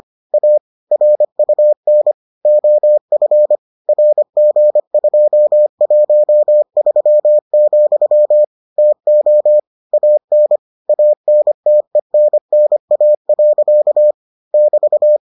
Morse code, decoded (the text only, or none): A RUN OF RG213, TO AN ANTENNA. <BT>